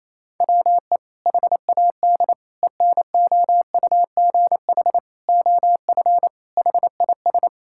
Morse code, heard as a keyed tone, about 28 wpm, 700 Hz. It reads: WE HAD ENOUGH OF HIS
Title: Sentences from Top 1600 Words 28wpm